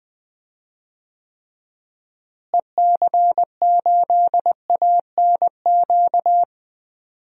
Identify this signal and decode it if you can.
Morse code, heard as a keyed tone, about 20 wpm, 700 Hz. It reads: EC8ANQ
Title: DXCC-100 Call Signs - 6 Characters - Repeated 2x - 20wpm